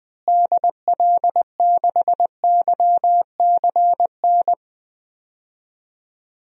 Morse code, heard as a keyed tone, about 20 wpm, 700 Hz. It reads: DL6YCN
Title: DXCC-100 Call Signs - 6 Characters - Repeated 3x - 20wpm